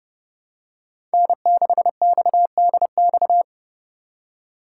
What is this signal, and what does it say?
Morse code, heard as a keyed tone, about 30 wpm, 700 Hz. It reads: N6XDX
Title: USA Call Signs - 2x Character Spacing - 30wpm